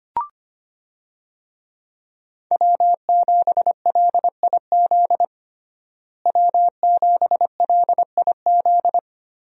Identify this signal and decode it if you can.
Morse code, heard as a keyed tone, about 25 wpm, 700 Hz. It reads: W7LIZ W7LIZ
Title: USA Call Signs - 5 Characters - Repeated 2x - 25wpm